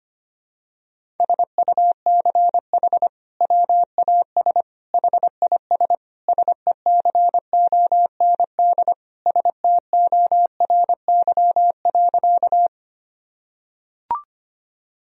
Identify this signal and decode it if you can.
Morse code, heard as a keyed tone, about 25 wpm, 700 Hz. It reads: SUCH WAS HIS SECOND STORY.
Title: Sentences from Top 1000 Words - Review Part 1 - 25wpm